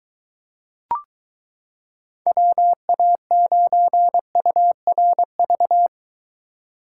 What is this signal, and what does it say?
Morse code, heard as a keyed tone, about 23 wpm, 700 Hz. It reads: WA9URV